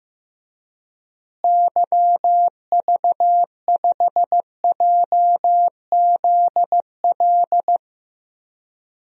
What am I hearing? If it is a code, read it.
Morse code, heard as a keyed tone, about 15 wpm, 700 Hz. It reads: YV5JZL
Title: DXCC-100 Call Signs - 6 Characters - Repeated 2x - 15wpm